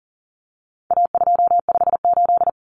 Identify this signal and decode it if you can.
Morse code, heard as a keyed tone, about 40 wpm, 700 Hz. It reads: A258
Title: Common Cut Numbers - 4-Digit 40wpm